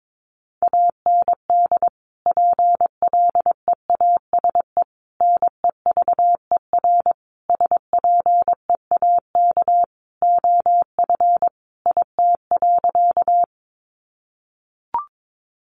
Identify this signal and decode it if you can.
Morse code, heard as a keyed tone, about 22 wpm, 700 Hz. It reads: AND PLEASE NEVER SPEAK OF IT.
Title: Sentences from Top 1500 Words 22wpm